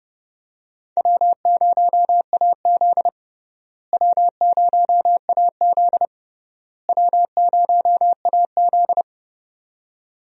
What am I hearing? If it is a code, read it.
Morse code, heard as a keyed tone, about 32 wpm, 700 Hz. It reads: W0AZ W0AZ W0AZ
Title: USA Call Signs - 4 Characters - Repeated 3x - 30wpm